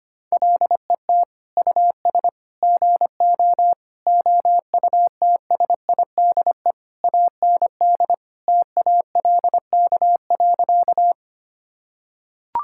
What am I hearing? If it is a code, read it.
Morse code, heard as a keyed tone, about 25 wpm, 700 Hz. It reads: LET US GO OUTSIDE AND TALK.
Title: Sentences from Top 1000 Words - Review Part 6 - 25wpm